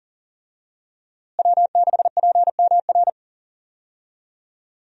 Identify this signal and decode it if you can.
Morse code, heard as a keyed tone, about 40 wpm, 700 Hz. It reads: W6PMR
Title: USA Call Signs - 5 Characters - Repeated 3x - 40wpm